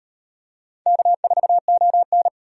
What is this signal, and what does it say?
Morse code, heard as a keyed tone, about 38 wpm, 700 Hz. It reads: K4ON